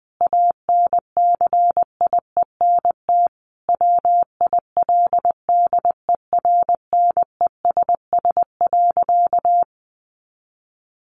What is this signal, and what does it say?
Morse code, heard as a keyed tone, about 20 wpm, 700 Hz. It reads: ANCIENT WILDERNESS.
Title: A Sound Of Thunder 20wpm